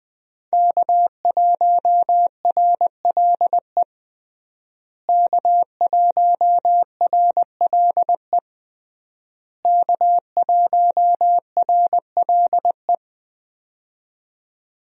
Morse code, heard as a keyed tone, about 20 wpm, 700 Hz. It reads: K1RLE K1RLE K1RLE
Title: USA Call Signs - 5 Characters - Repeated 3x - 20wpm